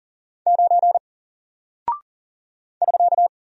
Morse code, heard as a keyed tone, about 39 wpm, 700 Hz. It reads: 9 <SK>
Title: Single Character (Letter:Number:Punctuation:Prosign) 40wpm